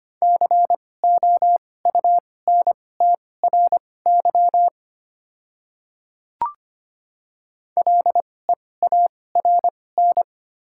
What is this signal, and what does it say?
Morse code, heard as a keyed tone, about 25 wpm, spaced as for 18 wpm, 700 Hz. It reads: COUNTRY LEARN
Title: 2x Character Spacing - Top 300 Words - 25wpm